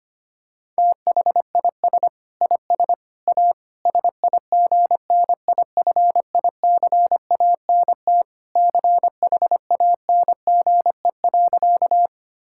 Morse code, heard as a keyed tone, about 25 wpm, 700 Hz. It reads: THIS IS A SIGNIFICANT CHANGE.
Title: Sentences from Top 1000 Words - Review Part 6 - 25wpm